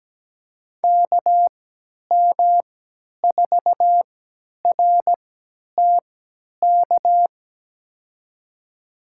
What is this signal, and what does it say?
Morse code, heard as a keyed tone, about 17 wpm, spaced as for 10 wpm, 700 Hz. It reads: KM4RTK
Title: USA Call Signs - 6 Characters - 3x Character Spacing - 17wpm